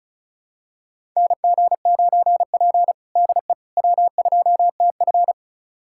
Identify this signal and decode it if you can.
Morse code, heard as a keyed tone, about 36 wpm, 690 Hz. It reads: NG9P DE W2TF
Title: USA Callsign DE Callsign - 4 Characters - 35wpm